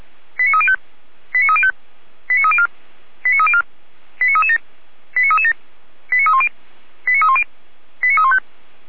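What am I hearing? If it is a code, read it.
Five-tone selective call (ZVEI), heard as quick runs of five tones, tones of about 80 ms. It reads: 89305 89305 89304 89304 89337 89337 89310 89310 89315
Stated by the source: ZVEI five-tone selective call received by SDR.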